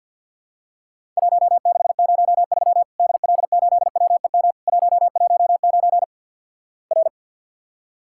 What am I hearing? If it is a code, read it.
Morse code, heard as a keyed tone, about 50 wpm, 700 Hz. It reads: 1603 DL8WEM 119 R
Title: Worked All Europe DX Contest - QTC Exchange - 50wpm